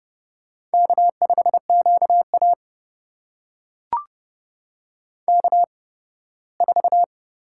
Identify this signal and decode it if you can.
Morse code, heard as a keyed tone, about 30 wpm, spaced as for 18 wpm, 700 Hz. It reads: K5QA K4
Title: USA Call Signs - 4 Characters - 8x Character Spacing - 30wpm